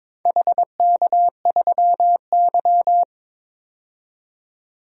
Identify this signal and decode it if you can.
Morse code, heard as a keyed tone, about 22 wpm, 700 Hz. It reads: HK3Y
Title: DXCC-100 Call Signs - 4 Characters - 22wpm